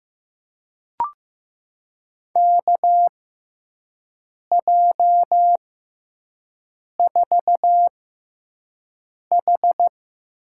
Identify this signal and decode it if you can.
Morse code, heard as a keyed tone, about 15 wpm, spaced as for 5 wpm, 700 Hz. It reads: KJ4H
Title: USA Call Signs - 6 Characters - 6x Character Spacing - 15wpm